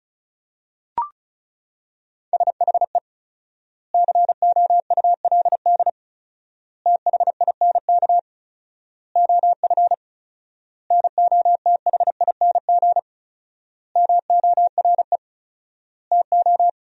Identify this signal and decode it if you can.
Morse code, heard as a keyed tone, about 34 wpm, 700 Hz. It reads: SHE COULD THINK OF NOTHING MORE TO
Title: 4x Word-Spacing - Sentences from Top 500 Words - 35wpm